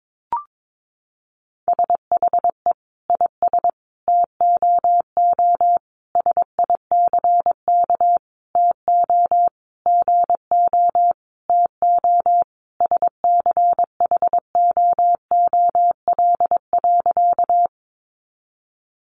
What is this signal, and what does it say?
Morse code, heard as a keyed tone, about 22 wpm, 700 Hz. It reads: SHE IS TOO SICK TO GO TO SCHOOL.
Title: Easy Phrases 22wpm